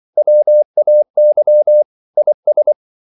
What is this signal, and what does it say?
Morse code, heard as a keyed tone, about 24 wpm, 600 Hz. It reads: WAY IS
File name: CW / Practice / EnglishQuotations0224